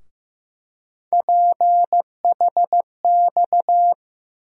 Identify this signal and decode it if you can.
Morse code, heard as a keyed tone, about 15 wpm, 700 Hz. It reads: PHX
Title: USA Call Sign Postfixes - 3 Letters 15wpm